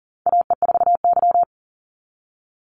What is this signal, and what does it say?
Morse code, heard as a keyed tone, about 40 wpm, 700 Hz. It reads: AE4Y
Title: NA SOTA Chaser Callsigns 40wpm